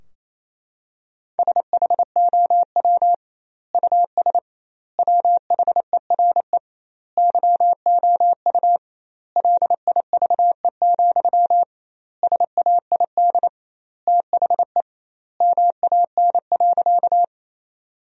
Morse code, heard as a keyed tone, about 28 wpm, 700 Hz. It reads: SHOW US WHERE YOU LIVE, SAID THE MAN.
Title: 2x Word-Spacing - Sentences from Top 200 Words - 28wpm